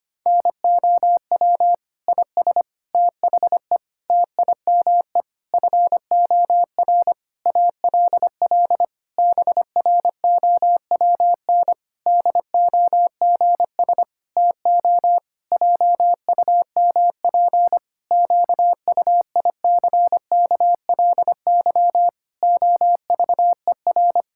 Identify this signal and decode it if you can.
Morse code, heard as a keyed tone, about 25 wpm, 700 Hz. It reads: NOW IS THE TIME FOR ALL BROWN DOGS TO JUMP QUICKLY OVER
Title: Pangram Challenge v2 - 25wpm